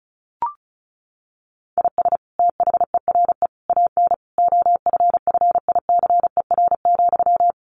Morse code, heard as a keyed tone, about 34 wpm, 700 Hz. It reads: IS THERE AN OFFICER,
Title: Sentences from Top 1200 Words 35wpm